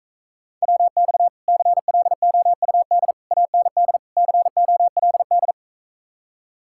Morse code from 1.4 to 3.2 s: CLOUD

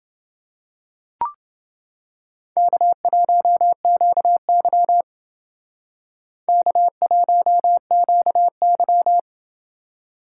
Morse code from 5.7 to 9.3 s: K1QY

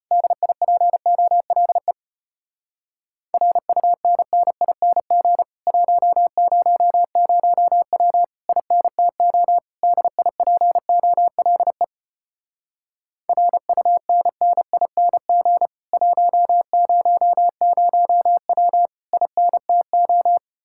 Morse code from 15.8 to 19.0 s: 100W

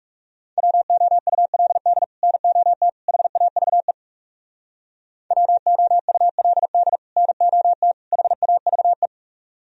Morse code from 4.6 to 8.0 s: WOULD NOT

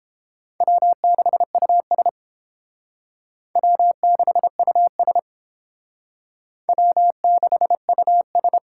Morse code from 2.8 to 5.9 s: W6US

